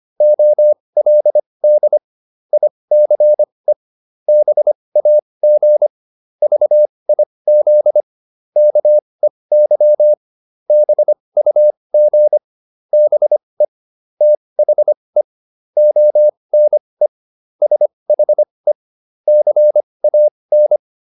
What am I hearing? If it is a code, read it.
Morse code, heard as a keyed tone, about 25 wpm, spaced as for 20 wpm, 600 Hz. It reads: OLD ICE BAG VIZ KEY BUG BE THE ONE SHE CAN